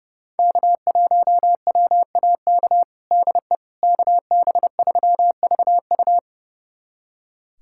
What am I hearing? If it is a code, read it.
Morse code, heard as a keyed tone, about 30 wpm, 700 Hz. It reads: K1WAK DE KB3VU
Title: USA Callsign DE Callsign - 5 Characters - 30wpm